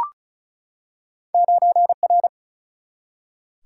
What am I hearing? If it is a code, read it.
Morse code, heard as a keyed tone, about 36 wpm, 700 Hz. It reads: 9R
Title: Single-Letter Single-Number 35wpm